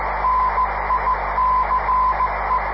Morse code, heard as a keyed tone, about 15 wpm, 1020 Hz, with radio noise. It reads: NICE
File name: 500khz / 30 december 2011 / iq2mi qso-1